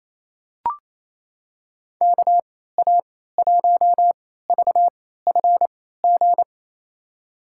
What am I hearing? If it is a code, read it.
Morse code, heard as a keyed tone, about 28 wpm, spaced as for 16 wpm, 700 Hz. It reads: KA1VFG
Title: USA Call Signs - 3x Character Spacing - 28wpm